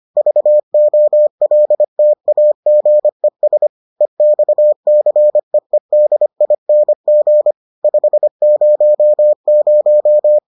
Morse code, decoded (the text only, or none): VOLTAGES EXCEEDING 500